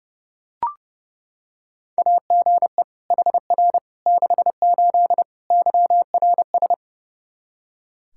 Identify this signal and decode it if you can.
Morse code, heard as a keyed tone, about 32 wpm, 700 Hz. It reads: AGE HR 68 YRS
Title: K1USN SST QSO Practice 30wpm